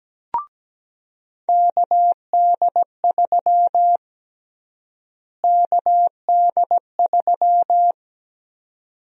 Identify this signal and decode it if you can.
Morse code, heard as a keyed tone, about 17 wpm, 700 Hz. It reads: KD3 KD3
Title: USA Call Sign Prefixes - Repeated 2x - 17wpm